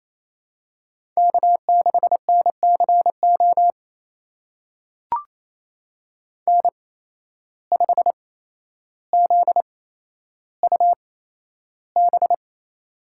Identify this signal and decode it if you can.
Morse code, heard as a keyed tone, about 28 wpm, spaced as for 12 wpm, 700 Hz. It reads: K6NCO N5ZUB
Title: USA Call Signs - 5 Characters - 8x Character Spacing - 28wpm